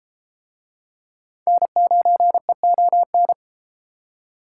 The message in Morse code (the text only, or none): N9EON